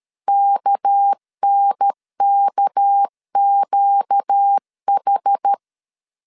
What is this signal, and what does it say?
Morse code, heard as a keyed tone, about 13 wpm, 800 Hz. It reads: KNKQH
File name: LEZIONI / 6 PKX0 / 6 PK 12wpm 2006